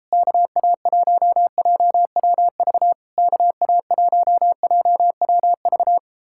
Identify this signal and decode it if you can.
Morse code, heard as a keyed tone, about 38 wpm, 700 Hz. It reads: KA1JWV KA1JWV